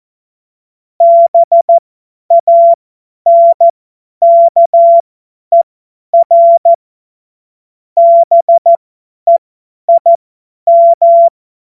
Morse code, heard as a keyed tone, about 14 wpm, spaced as for 10 wpm, 680 Hz. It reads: BANKER BEIM
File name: Morsekurs / Witze / Witz 24 Juni 2025 50 F70